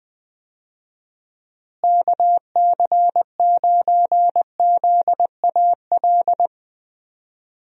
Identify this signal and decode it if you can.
Morse code, heard as a keyed tone, about 20 wpm, 700 Hz. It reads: KC9ZAL
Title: USA Call Signs - 6 Characters - Repeated 2x - 20wpm